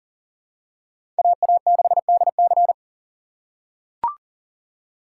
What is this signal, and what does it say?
Morse code, heard as a keyed tone, about 40 wpm, 700 Hz. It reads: AA6DC
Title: USA Call Signs - 5 Characters - 4x Character Spacing - 40wpm